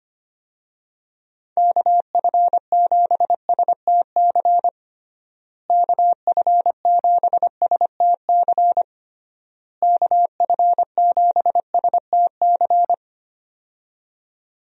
Morse code, heard as a keyed tone, about 25 wpm, 700 Hz. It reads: KF7STC KF7STC KF7STC
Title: USA Call Signs - 6 Characters - Repeated 3x - 25wpm